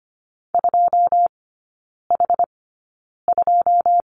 Morse code, heard as a keyed tone, about 25 wpm, spaced as for 9 wpm, 690 Hz. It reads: 2H2
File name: Lesson 14e - Number 2 - Three Characters 25wpm